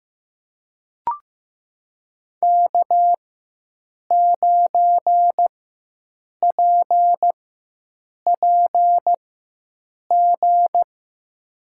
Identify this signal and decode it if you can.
Morse code, heard as a keyed tone, about 15 wpm, spaced as for 7 wpm, 700 Hz. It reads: K9PPG
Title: USA Call Signs - 5 Characters - 4x Character Spacing - 15wpm